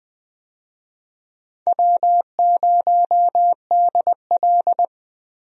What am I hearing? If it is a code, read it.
Morse code, heard as a keyed tone, about 21 wpm, 700 Hz. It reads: W0DL
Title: USA Call Signs - 4 Characters - Repeated 2x - 20wpm